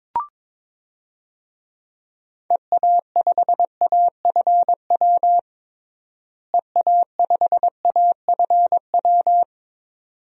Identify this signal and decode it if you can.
Morse code, heard as a keyed tone, about 22 wpm, 700 Hz. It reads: EA5AFW EA5AFW
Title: DXCC-100 Call Signs - 6 Characters - Repeated 2x - 22wpm